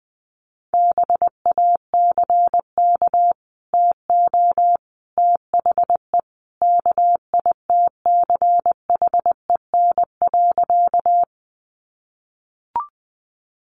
Morse code, heard as a keyed tone, about 20 wpm, 700 Hz. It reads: BACK TO THE KITCHEN.
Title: Sentences from Top 1400 Words 20wpm